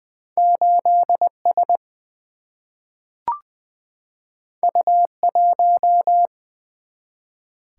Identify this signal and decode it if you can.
Morse code, heard as a keyed tone, about 20 wpm, 700 Hz. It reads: 8S U1
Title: Single-Letter Single-Number 20wpm